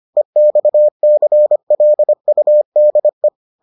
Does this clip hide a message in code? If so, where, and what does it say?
Yes: Morse code from 0.2 to 3.3 s, heard as a keyed tone, about 25 wpm, 600 Hz. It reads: EXCLUDE